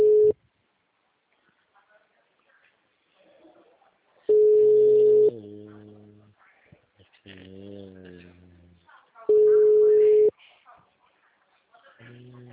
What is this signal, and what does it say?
Ringback tone (Europe, 425 Hz), with speech.